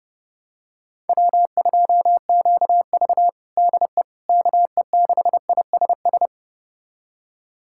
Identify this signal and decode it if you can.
Morse code, heard as a keyed tone, about 30 wpm, 700 Hz. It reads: W2QV DE KE6ISS
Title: USA Callsign DE Callsign 30wpm